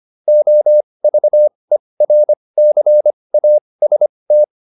Morse code, heard as a keyed tone, about 25 wpm, spaced as for 20 wpm, 600 Hz. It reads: OVERCAST
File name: QSO406 20